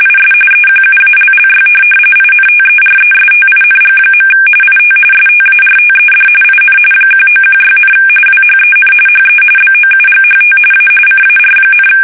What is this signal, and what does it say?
RTTY, about 45.45 baud. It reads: IS EQUALLY INTERESTING. IT STARTED IN 1916, WHEN HE BOUGHT A CAR,
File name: fsk850 / benjamin-g-lamme